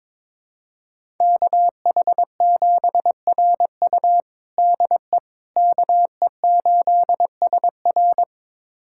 Morse code, heard as a keyed tone, about 22 wpm, 700 Hz. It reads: KH7RU DE KE8SR
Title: USA Callsign DE Callsign - 5 Characters - 22wpm